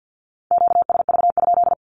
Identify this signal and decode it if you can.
Morse code, heard as a keyed tone, about 50 wpm, 700 Hz. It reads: QSV?
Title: Q-Codes with Spoken Meaning - 50wpm